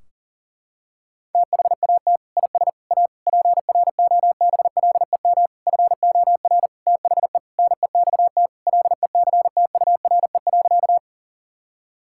Morse code, heard as a keyed tone, about 40 wpm, 700 Hz. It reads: THAT IS A PROBLEM FOR THE NEXT LECTURE.
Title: Sentences from Top 1600 Words 40wpm